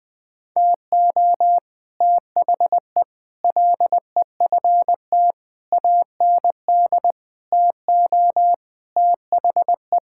Morse code, heard as a keyed tone, about 21 wpm, 700 Hz. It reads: TO THE LEFT AND TO THE
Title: Sets of 8 Words - 20wpm